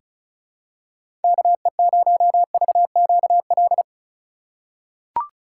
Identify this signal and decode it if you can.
Morse code, heard as a keyed tone, about 36 wpm, 700 Hz. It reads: KE0VQL
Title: USA Call Signs - 6 Characters - 3x Character Spacing - 35wpm